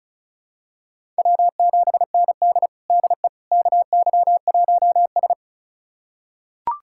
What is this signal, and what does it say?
Morse code, heard as a keyed tone, about 36 wpm, 700 Hz. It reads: W7ND DE KY1S
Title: USA Callsign DE Callsign - 4 Characters - 35wpm